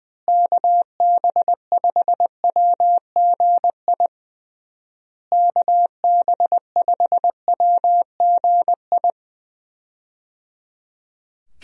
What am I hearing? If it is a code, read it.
Morse code, heard as a keyed tone, about 20 wpm, 700 Hz. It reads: KB5WGI KB5WGI
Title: USA Call Signs - 6 Characters - Repeated 3x - 20wpm